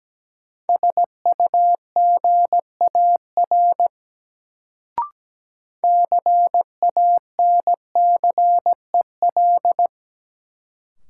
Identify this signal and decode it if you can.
Morse code, heard as a keyed tone, about 17 wpm, 700 Hz. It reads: SUGAR CANCEL
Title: Sentences from Top 1200 Words - New Word Drill - 17wpm